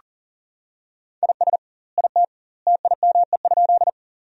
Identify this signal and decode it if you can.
Morse code, heard as a keyed tone, about 40 wpm, 700 Hz. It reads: IS IT TIME?
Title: 2x Word-Spacing - Sentences from Top 100 Words - 40wpm